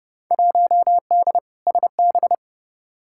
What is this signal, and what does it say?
Morse code, heard as a keyed tone, about 30 wpm, 700 Hz. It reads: 1D SB
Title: Field Day Exchanges - 30wpm